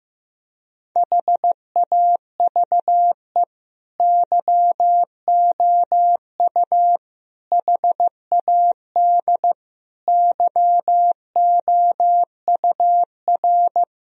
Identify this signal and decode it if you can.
Morse code, heard as a keyed tone, about 15 wpm, 700 Hz. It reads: HAVE YOU HAD YOUR
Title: Sentences from Top 1000 Words - Review Part 3 - 15wpm